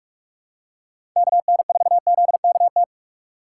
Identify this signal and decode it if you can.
Morse code, heard as a keyed tone, about 44 wpm, 700 Hz. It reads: KN4ZKT
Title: K1USN SST QSO Practice - 45wpm